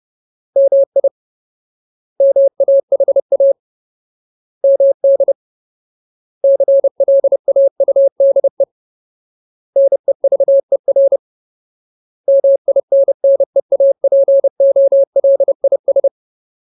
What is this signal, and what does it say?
Morse code, heard as a keyed tone, about 30 wpm, 550 Hz. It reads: MI MAHA MD CLAUDE NEVER MINNEAPOLIS
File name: Ultimate Word List / 30WPM Ultimate